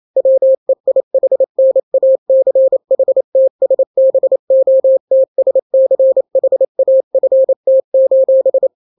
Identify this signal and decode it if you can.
Morse code, heard as a keyed tone, about 28 wpm, 530 Hz. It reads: WEIHNACHTSBOTSCHAFT:
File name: daily / 27wpm